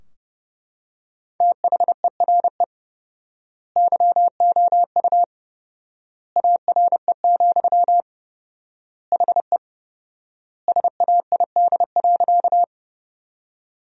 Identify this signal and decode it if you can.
Morse code, heard as a keyed tone, about 30 wpm, 700 Hz. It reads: THERE YOU ARE, HE SAID.
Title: 4x Word-Spacing - Sentences from Top 100 Words - 30wpm